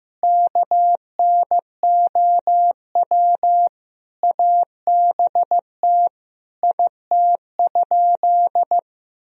Morse code, heard as a keyed tone, about 16 wpm, 700 Hz. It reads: KNOW ABT IT?
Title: Sentences from Top 100 Words and Abbreviations 15wpm